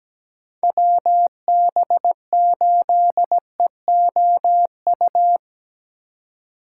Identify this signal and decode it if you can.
Morse code, heard as a keyed tone, about 20 wpm, 700 Hz. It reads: WB8EOU
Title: Speed Racing - USA 2x3 Call Signs 17wpm